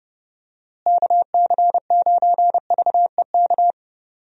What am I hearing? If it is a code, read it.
Morse code, heard as a keyed tone, about 32 wpm, 700 Hz. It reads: KC9VEK
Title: USA Call Signs - 6 Characters - Repeated 3x - 30wpm